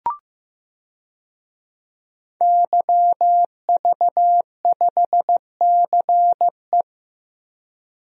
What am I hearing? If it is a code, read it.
Morse code, heard as a keyed tone, about 15 wpm, 700 Hz. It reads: YV5CE
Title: DXCC-100 Call Signs - 5 Characters - 15wpm